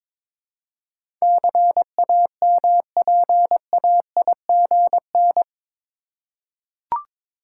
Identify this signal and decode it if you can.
Morse code, heard as a keyed tone, about 21 wpm, 700 Hz. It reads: CAMPAIGN
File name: practice / 22wpm / Eight Letter Words - 22wpm